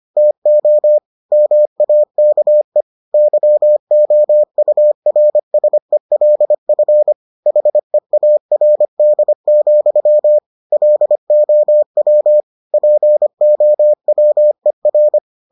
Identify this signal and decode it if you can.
Morse code, heard as a keyed tone, about 25 wpm, 600 Hz. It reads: TO MAKE YOURSELF HEARD, LOW POWER